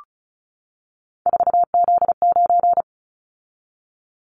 Morse code, heard as a keyed tone, about 36 wpm, 700 Hz. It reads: 4Z9
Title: Callsign Prefixes for Top 50 DXCC - 35wpm